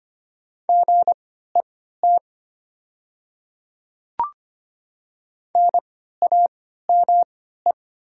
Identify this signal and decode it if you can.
Morse code, heard as a keyed tone, about 25 wpm, spaced as for 14 wpm, 700 Hz. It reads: GET NAME
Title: 3x Character Spacing - Top 300 Words - 25wpm